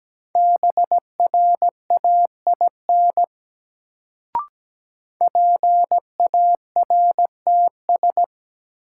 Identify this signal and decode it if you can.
Morse code, heard as a keyed tone, about 17 wpm, 700 Hz. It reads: BRAIN PARTS
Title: Sentences from Top 1000 Words - Review Part 6 - New Word Drill - 17wpm